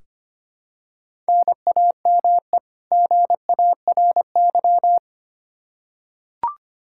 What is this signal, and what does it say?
Morse code, heard as a keyed tone, about 25 wpm, 700 Hz. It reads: NAME GARY
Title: Practice QSO - Exchange 1 - Beginning - 25wpm